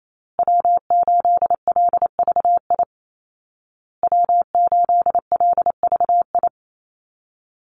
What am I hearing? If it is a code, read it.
Morse code, heard as a keyed tone, about 28 wpm, 700 Hz. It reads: W8LVI W8LVI
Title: Call Signs Repeated 2x - 28wpm